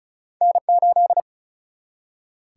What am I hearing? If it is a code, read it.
Morse code, heard as a keyed tone, about 34 wpm, 700 Hz. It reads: N 8
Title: Single-Letter Single-Number 35wpm